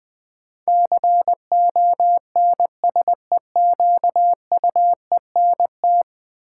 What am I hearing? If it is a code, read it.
Morse code, heard as a keyed tone, about 20 wpm, 700 Hz. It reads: CONSEQUENT
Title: Build A Word - 5 Letters At A Time - 12-Letter Words - 20wpm